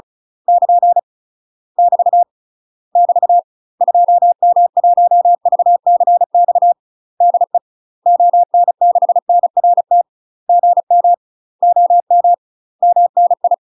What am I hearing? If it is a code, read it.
Morse code, heard as a keyed tone, about 36 wpm, 700 Hz. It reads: <KN> <BT> <BT> 2M1VCX DE ON6NRT GM OM MNI
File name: UK MASTER QSOs TRAINING / 35WPM / UK MASTER QSOs 0033